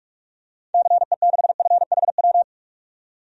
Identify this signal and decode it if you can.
Morse code, heard as a keyed tone, about 46 wpm, 700 Hz. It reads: CE6FHW
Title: DXCC-100 Call Signs - 6 Characters - 45wpm